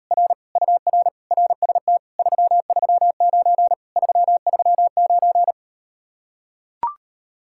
Morse code, heard as a keyed tone, about 38 wpm, 700 Hz. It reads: R UR RST 339 339